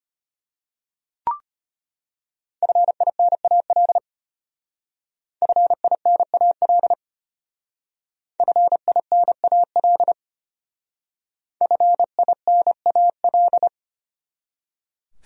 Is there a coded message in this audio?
Yes: Morse code from 2.6 to 13.7 s, heard as a keyed tone, about 32 wpm, 700 Hz. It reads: FINAL FINAL FINAL FINAL